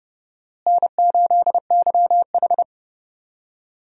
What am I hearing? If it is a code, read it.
Morse code, heard as a keyed tone, about 32 wpm, 700 Hz. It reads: N8YH
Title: USA Call Signs - 4 Characters - Repeated 2x - 30wpm